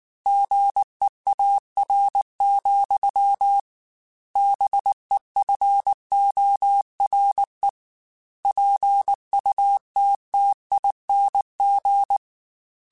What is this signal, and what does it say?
Morse code, heard as a keyed tone, about 19 wpm, 800 Hz. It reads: GEAR, BEFORE PUTTING
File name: Chuckles / Chuckles-mp3s / BrainMouth @18